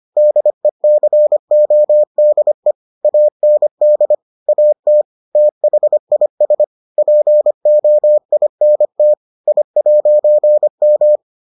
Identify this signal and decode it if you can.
Morse code, heard as a keyed tone, about 25 wpm, 600 Hz. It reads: DECODE AND AT THIS POINT I'M